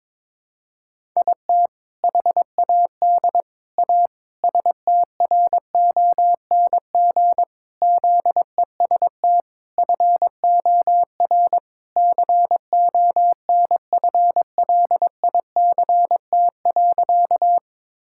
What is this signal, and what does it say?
Morse code, heard as a keyed tone, about 22 wpm, 700 Hz. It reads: IT HAD A STRONG ZEST FOR CONFLICT.
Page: morse-code-ninja-practice-22wpm